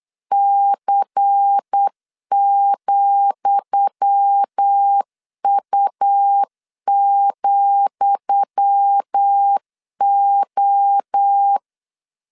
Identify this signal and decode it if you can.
Morse code, heard as a keyed tone, about 8 wpm, 800 Hz. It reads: C,U,O
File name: LEZIONI / 10 SIGNS / 10D COMA 8wpm 3749